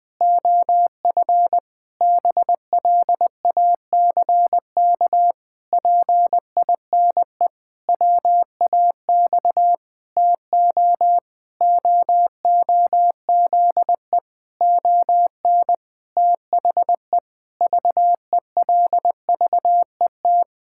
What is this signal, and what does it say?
Morse code, heard as a keyed tone, about 20 wpm, 700 Hz. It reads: OF BLACK PINE WAX TO OOZE ON THE VELVET
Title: Pangram Challenge v2 - 20wpm